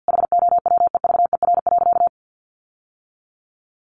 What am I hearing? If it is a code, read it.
Morse code, heard as a keyed tone, about 50 wpm, 700 Hz. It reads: HOWEVER.
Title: Sentences from Top 1400 Words 50wpm